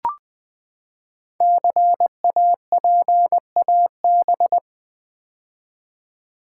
Morse code, heard as a keyed tone, about 20 wpm, 700 Hz. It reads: CAPAB
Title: Build A Word - 5 Letters At A Time - 10-Letter Words - 20wpm